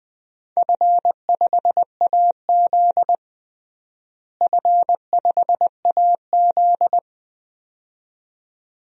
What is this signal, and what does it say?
Morse code, heard as a keyed tone, about 21 wpm, 700 Hz. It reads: F5AZ F5AZ
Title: DXCC-100 Call Signs - 4 Characters - Repeated 2x - 20wpm